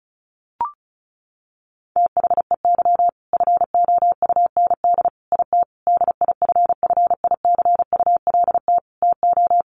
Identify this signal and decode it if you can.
Morse code, heard as a keyed tone, about 34 wpm, 700 Hz. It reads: THEY FOUND IT DIFFICULT TO
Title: Sentences from Top 1400 Words 35wpm